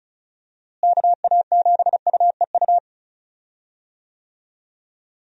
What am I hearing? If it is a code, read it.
Morse code, heard as a keyed tone, about 34 wpm, 700 Hz. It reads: KA7UEU
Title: USA Call Signs - 6 Characters - Repeated 2x - 35wpm